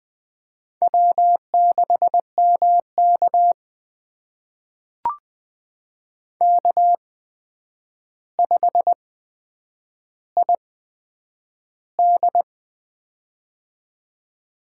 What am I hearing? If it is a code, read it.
Morse code, heard as a keyed tone, about 20 wpm, spaced as for 9 wpm, 700 Hz. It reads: W6MK K5ID
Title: USA Call Signs - 4 Characters - 8x Character Spacing - 20wpm